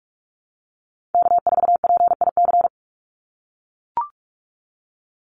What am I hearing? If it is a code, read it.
Morse code, heard as a keyed tone, about 44 wpm, 700 Hz. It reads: K4PIC
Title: NA SOTA Chaser Callsigns 45wpm